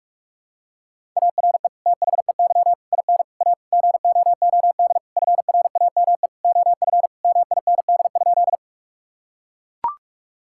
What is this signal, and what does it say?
Morse code, heard as a keyed tone, about 41 wpm, 700 Hz. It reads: ARE THEY IN A GOOD FRAME OF MIND?
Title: Sentences from Top 1000 Words - Review Part 6 - 45wpm